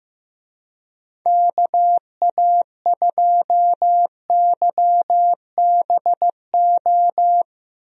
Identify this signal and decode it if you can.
Morse code, heard as a keyed tone, about 15 wpm, 700 Hz. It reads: KA2YBO
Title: USA Call Signs - 6 Characters - Repeated 2x - 15wpm